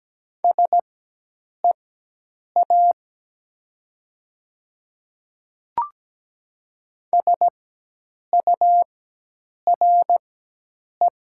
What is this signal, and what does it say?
Morse code, heard as a keyed tone, about 17 wpm, spaced as for 8 wpm, 700 Hz. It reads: SEA SURE
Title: 4x Character Spacing - Top 300 Words - 17wpm